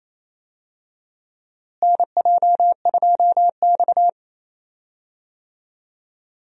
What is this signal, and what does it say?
Morse code, heard as a keyed tone, about 28 wpm, 700 Hz. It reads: NJ2X
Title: USA Call Signs - 4 Characters - Repeated 3x - 28wpm